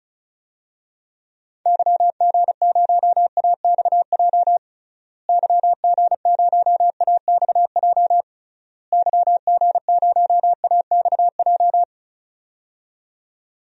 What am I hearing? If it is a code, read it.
Morse code, heard as a keyed tone, about 36 wpm, 700 Hz. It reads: YG0AXJ YG0AXJ YG0AXJ
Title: DXCC-100 Call Signs - 6 Characters - Repeated 3x - 35wpm